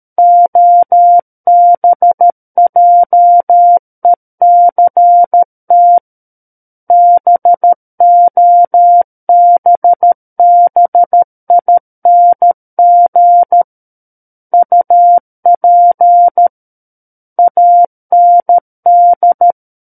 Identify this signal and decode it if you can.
Morse code, heard as a keyed tone, about 13 wpm, 700 Hz. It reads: OBJECT BOBBING UP AND
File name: Book One / 04-Chapter 4 - 13 WPM